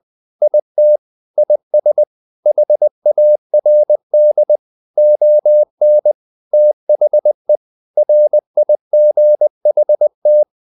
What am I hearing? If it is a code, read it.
Morse code, heard as a keyed tone, about 20 wpm, 600 Hz. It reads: IT IS HARD ON THE RIGHT
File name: the old man and the sea part 2 20wpm